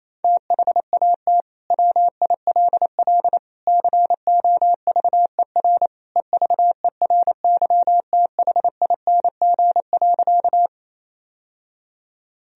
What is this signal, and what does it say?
Morse code, heard as a keyed tone, about 28 wpm, 700 Hz. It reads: THAT WILL COVER EVERYTHING.
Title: Sentences from Top 1000 Words - Review Part 1 - 28wpm